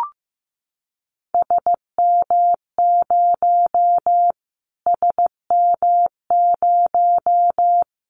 Morse code, heard as a keyed tone, about 15 wpm, 700 Hz. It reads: SM0 SM0
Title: Callsign Prefixes for Top 25 DXCC - Repeated 3x - 15wpm